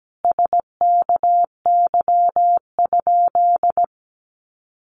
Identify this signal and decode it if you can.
Morse code, heard as a keyed tone, about 17 wpm, 700 Hz. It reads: SKY?
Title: Sentences from Top 1200 Words 17wpm